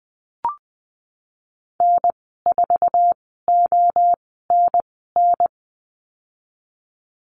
Morse code, heard as a keyed tone, about 20 wpm, spaced as for 15 wpm, 700 Hz. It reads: N4ONN
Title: 055 Lesson 10g - Letter r - Call Signs 20wpm